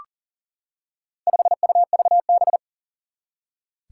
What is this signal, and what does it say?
Morse code, heard as a keyed tone, about 37 wpm, 700 Hz. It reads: 5UVB
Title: Four Challenging Letter-Number Combinations 40wpm